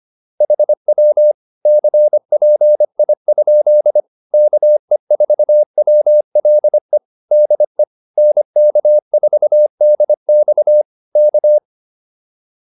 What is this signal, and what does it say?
Morse code, heard as a keyed tone, about 25 wpm, 600 Hz. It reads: HW CPI? KE4WLE DE NK4DX K